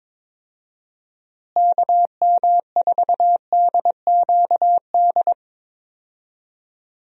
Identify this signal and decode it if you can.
Morse code, heard as a keyed tone, about 22 wpm, 700 Hz. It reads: KM4DQD
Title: USA Call Signs - 6 Characters - Repeated 3x - 22wpm